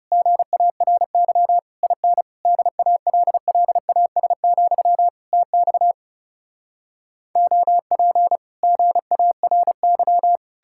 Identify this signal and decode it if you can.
Morse code, heard as a keyed tone, about 40 wpm, 700 Hz. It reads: GARY IN DALLAS, TX OP GARY